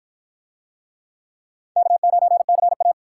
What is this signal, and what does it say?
Morse code, heard as a keyed tone, about 51 wpm, 700 Hz. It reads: K9CA